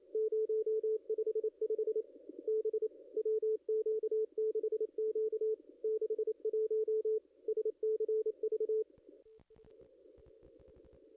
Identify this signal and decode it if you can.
Morse code, heard as a keyed tone, about 26 wpm, 380 Hz, with radio noise. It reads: 055 B WQ6Q 61 SCV
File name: contests / ss-cw-k5tr-op-n5rz -0200